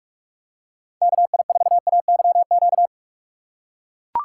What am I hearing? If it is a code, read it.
Morse code, heard as a keyed tone, about 40 wpm, 700 Hz. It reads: KI4AYQ